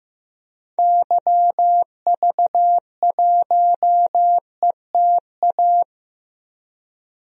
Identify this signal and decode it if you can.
Morse code, heard as a keyed tone, about 15 wpm, 700 Hz. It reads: YV1ETA
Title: DXCC-100 Call Signs - 6 Characters - 15wpm